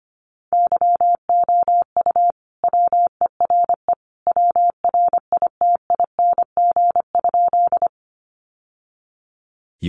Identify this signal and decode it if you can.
Morse code, heard as a keyed tone, about 25 wpm, 700 Hz. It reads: YOU WERE WRITING?
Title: Sentences from Top 800 Words 25wpm